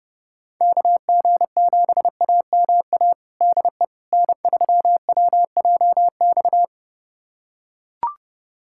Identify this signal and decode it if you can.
Morse code, heard as a keyed tone, about 30 wpm, 700 Hz. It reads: KG7AMA DE N3WJX
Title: USA Callsign DE Callsign 30wpm